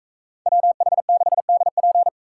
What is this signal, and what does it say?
Morse code, heard as a keyed tone, about 42 wpm, 700 Hz. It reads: WH6DP